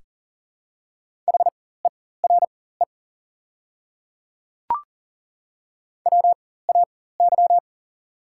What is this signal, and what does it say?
Morse code, heard as a keyed tone, about 40 wpm, spaced as for 19 wpm, 690 Hz. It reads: HERE WAY
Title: 4x Character Spacing - Top 300 Words - 40wpm